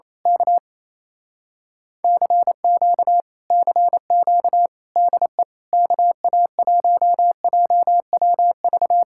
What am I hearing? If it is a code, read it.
Morse code, heard as a keyed tone, about 26 wpm, 700 Hz. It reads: K CQ CQ DE KA1JWV
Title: Speed-Racing Common QSO Phrases - 28wpm